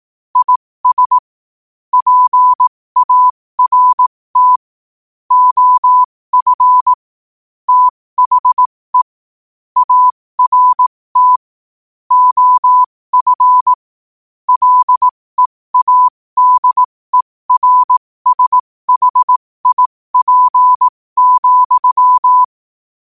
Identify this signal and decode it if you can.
Morse code, heard as a keyed tone, about 18 wpm, spaced as for 15 wpm, 1000 Hz. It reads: IS PART OF THE ART OF LEADERSHIP,
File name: qotdfiles / 15wpm